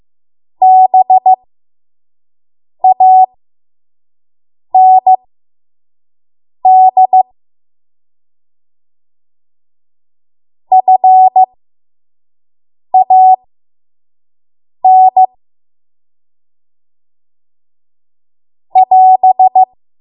Morse code, heard as a keyed tone, about 15 wpm, spaced as for 5 wpm, 750 Hz. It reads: BAND FAN <AS>
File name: Morse / 05WPM